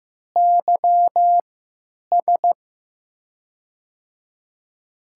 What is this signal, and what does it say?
Morse code, heard as a keyed tone, about 15 wpm, spaced as for 9 wpm, 700 Hz. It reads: YS
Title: USA Call Signs - 5 Characters - 3x Character Spacing - 15wpm